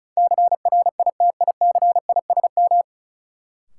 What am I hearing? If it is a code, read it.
Morse code, heard as a keyed tone, about 36 wpm, 700 Hz. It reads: CRITICISM
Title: Sentences from Top 1300 Words - New Word Drill - 35wpm